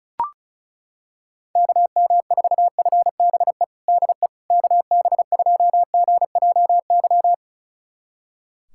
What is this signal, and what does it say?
Morse code, heard as a keyed tone, about 34 wpm, 700 Hz. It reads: KM4FBE DE KB2GJY
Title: USA Callsign DE Callsign - 6 Characters - 35wpm